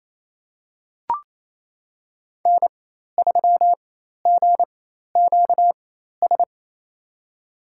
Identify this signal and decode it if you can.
Morse code, heard as a keyed tone, about 28 wpm, spaced as for 13 wpm, 700 Hz. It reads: N3GQS
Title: USA Call Signs - 5 Characters - 4x Character Spacing - 28wpm